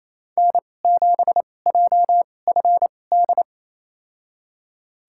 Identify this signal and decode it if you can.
Morse code, heard as a keyed tone, about 28 wpm, spaced as for 20 wpm, 700 Hz. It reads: N7JFD
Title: USA Call Signs - 5 Characters - 2x Character Spacing - 28wpm